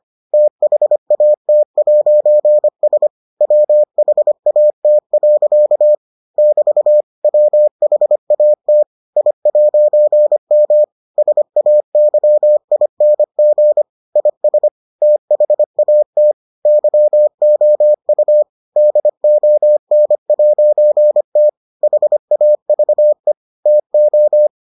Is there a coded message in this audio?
Yes: Morse code from 0.3 to 24.5 s, heard as a keyed tone, about 25 wpm, 600 Hz. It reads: THAT'S WHAT. <BT> WHAT I'M SAYING IS THAT YOU DON'T HAVE TO